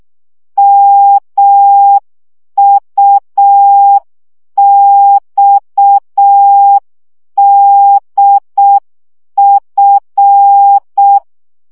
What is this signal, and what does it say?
Morse code, heard as a keyed tone, about 6 wpm, 800 Hz. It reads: MUXDF
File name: morze vizsga 2011-3 5